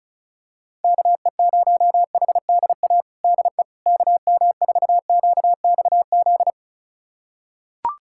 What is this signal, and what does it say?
Morse code, heard as a keyed tone, about 36 wpm, 700 Hz. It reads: KE0HDA DE KM4QXZ
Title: USA Callsign DE Callsign - 6 Characters - 35wpm